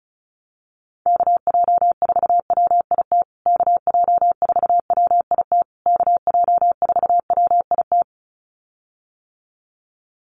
Morse code, heard as a keyed tone, about 34 wpm, 700 Hz. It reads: KJ4WIT KJ4WIT KJ4WIT
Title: USA Call Signs Repeated 3x 35wpm